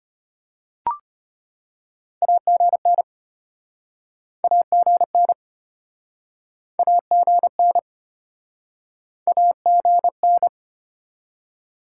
Morse code, heard as a keyed tone, about 35 wpm, 690 Hz. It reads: AGN AGN AGN AGN